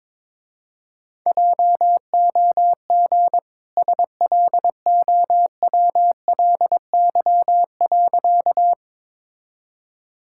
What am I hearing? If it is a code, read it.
Morse code, heard as a keyed tone, about 22 wpm, 700 Hz. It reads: JOG SLOWLY.
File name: practice / 22wpm / Easy Sentences with Hints - 22wpm